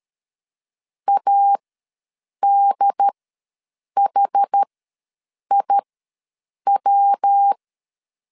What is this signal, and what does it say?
Morse code, heard as a keyed tone, about 13 wpm, spaced as for 7 wpm, 800 Hz. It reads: ADHIW
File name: LEZIONI / 5 LBVC / 5 L ws 12wpm 2958